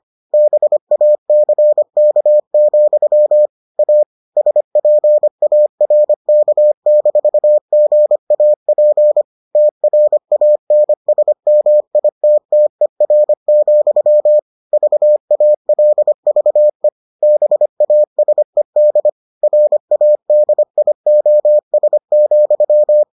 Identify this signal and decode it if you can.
Morse code, heard as a keyed tone, about 25 wpm, 600 Hz. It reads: BACK, A SPARK-GAP TRANSMITTER, VALVE BASED RADIOS,